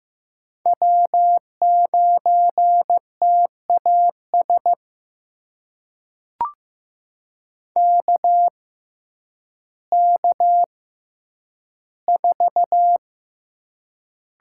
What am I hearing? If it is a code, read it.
Morse code, heard as a keyed tone, about 15 wpm, spaced as for 9 wpm, 700 Hz. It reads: W9TAS KK4
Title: USA Call Signs - 5 Characters - 6x Character Spacing - 15wpm